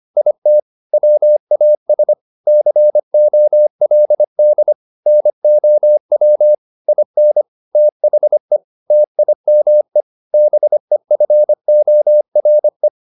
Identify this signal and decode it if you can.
Morse code, heard as a keyed tone, about 25 wpm, 600 Hz. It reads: IT WAS COLD NOW IN THE TIME BEFORE
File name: the old man and the sea part 1 25wpm